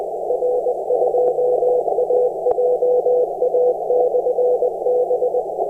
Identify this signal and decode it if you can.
Morse code, heard as a keyed tone, about 20 wpm, 500 Hz, with noise. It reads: R2AJA/B
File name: attachments / 16.23.03 r2ajab